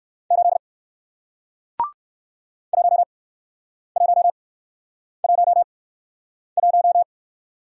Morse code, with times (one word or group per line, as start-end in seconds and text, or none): 0.3-0.6 8
2.7-7.0 1111